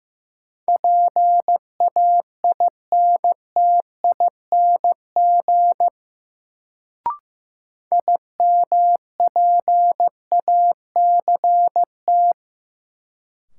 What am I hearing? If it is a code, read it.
Morse code, heard as a keyed tone, about 15 wpm, 700 Hz. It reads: PAINTING IMPACT
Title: Sentences from Top 1000 Words - Review Part 6 - New Word Drill - 15wpm